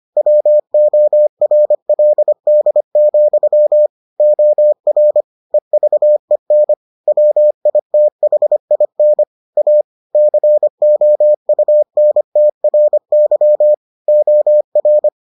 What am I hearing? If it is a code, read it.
Morse code, heard as a keyed tone, about 25 wpm, 600 Hz. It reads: WORLD, OR EVEN WITHIN A COUNTRY OR